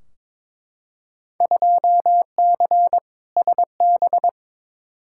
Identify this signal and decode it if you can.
Morse code, heard as a keyed tone, about 21 wpm, 700 Hz. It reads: 2C SB
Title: Field Day Exchanges Encore - 22wpm